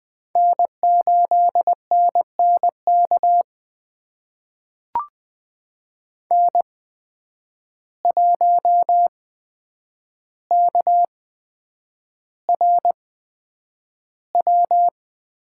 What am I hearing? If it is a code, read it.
Morse code, heard as a keyed tone, about 20 wpm, spaced as for 9 wpm, 700 Hz. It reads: N8NNK N1KRW
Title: USA Call Signs - 5 Characters - 8x Character Spacing - 20wpm